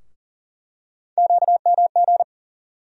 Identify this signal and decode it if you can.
Morse code, heard as a keyed tone, about 37 wpm, 700 Hz. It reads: QMG
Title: USA Call Sign Postfixes - 3 Letters 40wpm